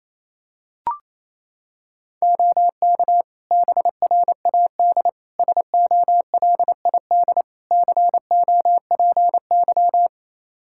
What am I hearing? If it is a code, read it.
Morse code, heard as a keyed tone, about 27 wpm, 700 Hz. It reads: OK BRAD SOLID COPY
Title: K1USN SST QSO Practice 28wpm